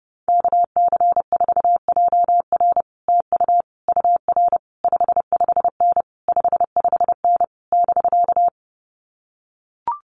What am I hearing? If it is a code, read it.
Morse code, heard as a keyed tone, about 28 wpm, 700 Hz. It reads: KC4JR TU UR 55N 55N <BK>
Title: Common QSO Phrases Encore 30wpm